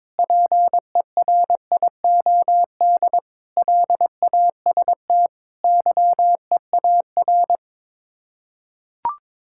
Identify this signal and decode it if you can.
Morse code, heard as a keyed tone, about 22 wpm, 700 Hz. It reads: PERIOD LAST YEAR
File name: practice / 22wpm / Sets of 7 Words - 22wpm